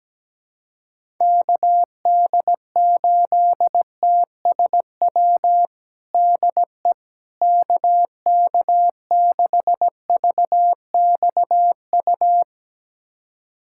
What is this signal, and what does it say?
Morse code, heard as a keyed tone, about 17 wpm, 700 Hz. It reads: KD8TSW DE KK6VXU
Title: USA Callsign DE Callsign - 6 Characters - 17wpm